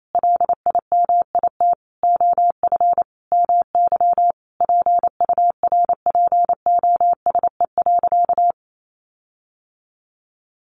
Morse code, heard as a keyed tone, about 28 wpm, 700 Hz. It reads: LIMIT OF MY PURPOSE.
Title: Sentences from Top 1500 Words 28wpm